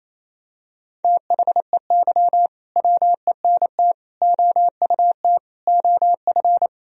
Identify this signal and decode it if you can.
Morse code, heard as a keyed tone, about 28 wpm, 700 Hz. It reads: THEY WENT OUT OF
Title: Sentences from Top 1000 Words - Review Part 4 - 28wpm